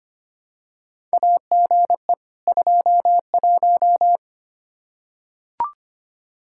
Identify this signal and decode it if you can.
Morse code, heard as a keyed tone, about 25 wpm, 700 Hz. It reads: AGE 21
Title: Practice QSO - Exchange 3 - More Info - 25wpm